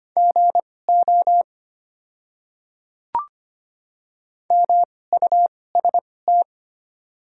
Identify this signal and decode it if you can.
Morse code, heard as a keyed tone, about 25 wpm, spaced as for 18 wpm, 700 Hz. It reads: GO MUST
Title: 2x Character Spacing - Top 200 Words - 25wpm